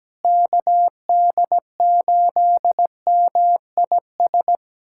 Morse code, heard as a keyed tone, about 17 wpm, 700 Hz. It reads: KD8MIS
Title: USA Callsign DE Callsign 17wpm